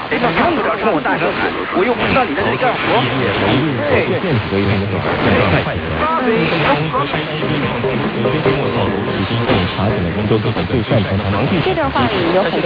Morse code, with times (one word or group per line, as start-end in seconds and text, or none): none